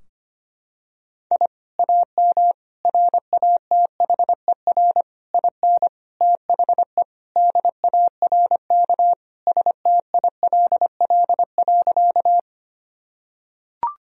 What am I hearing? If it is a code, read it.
Morse code, heard as a keyed tone, about 25 wpm, 700 Hz. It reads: I AM RATHER IN THE DARK STILL.
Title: Sentences from Top 1000 Words - Review Part 4 - 25wpm